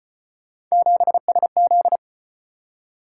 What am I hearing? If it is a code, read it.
Morse code, heard as a keyed tone, about 34 wpm, 700 Hz. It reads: 7SZ